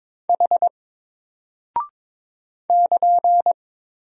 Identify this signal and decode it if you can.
Morse code, heard as a keyed tone, about 22 wpm, 700 Hz. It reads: H <KN>
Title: Single Character (Letter:Number:Punctuation:Prosign) Alt 22wpm